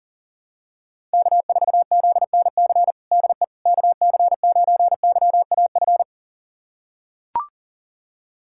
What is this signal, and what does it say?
Morse code, heard as a keyed tone, about 40 wpm, 700 Hz. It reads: K4ZNC DE KC9YAF
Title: USA Callsign DE Callsign 40wpm